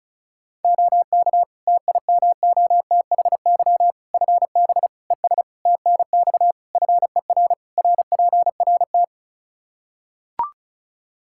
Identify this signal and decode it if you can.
Morse code, heard as a keyed tone, about 34 wpm, 700 Hz. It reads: OK TIMOTHY FB ES TNX FER RPRT
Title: K1USN SST QSO Practice 35wpm